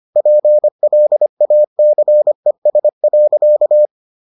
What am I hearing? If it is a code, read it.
Morse code, heard as a keyed tone, about 25 wpm, 600 Hz. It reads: PLACES.